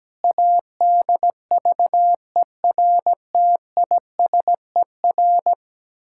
Morse code, heard as a keyed tone, about 17 wpm, 700 Hz. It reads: ADVERTISER
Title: Pangram Challenge v2 - 17wpm